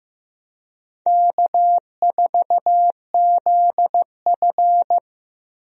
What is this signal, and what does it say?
Morse code, heard as a keyed tone, about 15 wpm, 700 Hz. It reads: K4ZF
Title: USA Call Signs - 4 Characters - 15wpm